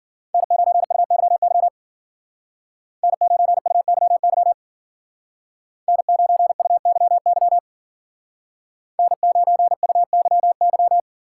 Morse code, heard as a keyed tone, about 49 wpm, 700 Hz. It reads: N9UYY N9UYY N9UYY N9UYY
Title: Speed Racing - USA 1x3 Call Signs 40wpm